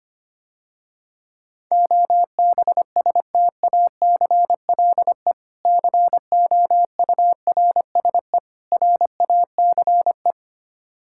Morse code, heard as a keyed tone, about 25 wpm, 690 Hz. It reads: OBSTACLE COURSE RACE
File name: practice / 25wpm / Difficult Words - Speed-Racing with Hints - 25wpm